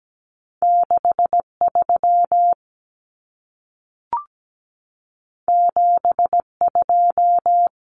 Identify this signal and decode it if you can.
Morse code, heard as a keyed tone, about 17 wpm, 700 Hz. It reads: 63 72
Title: Common Cut Numbers - 2-Digit 17wpm